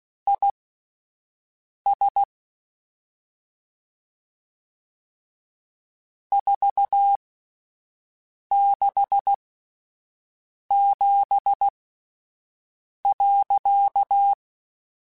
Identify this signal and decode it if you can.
Morse code, heard as a keyed tone, about 16 wpm, spaced as for 5 wpm, 800 Hz. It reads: IS 467.
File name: (spaced) / QSOs Spaced-mp3s / WA7QOX 2 @10 sp